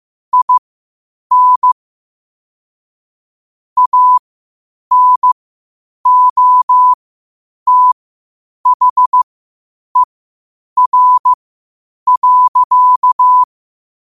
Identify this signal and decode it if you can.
Morse code, heard as a keyed tone, about 15 wpm, spaced as for 8 wpm, 1000 Hz. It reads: IN ANOTHER.
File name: qotdfiles / 8wpm